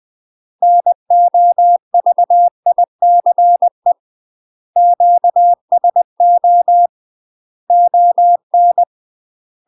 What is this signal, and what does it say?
Morse code, heard as a keyed tone, about 20 wpm, 700 Hz. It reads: NOVICE QSO ON
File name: UK MASTER QSOs TRAINING / 20WPM / UK MASTER QSOs 0068